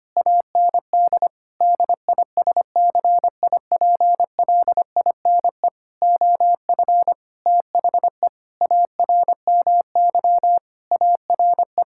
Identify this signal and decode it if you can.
Morse code, heard as a keyed tone, about 25 wpm, 700 Hz. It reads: AND DISCIPLINE OF THE ARMY ARE
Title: Sentences from Top 1000 Words - Review Part 6 - 25wpm